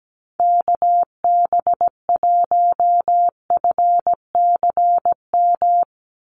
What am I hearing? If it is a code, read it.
Morse code, heard as a keyed tone, about 17 wpm, 700 Hz. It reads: KB1FCM
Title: USA Call Signs 17wpm